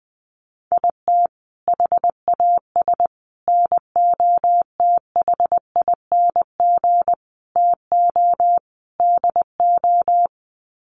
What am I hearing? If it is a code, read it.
Morse code, heard as a keyed tone, about 20 wpm, 700 Hz. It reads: IT HAS NOTHING TO DO
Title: Sets of 5 Words Encore 20wpm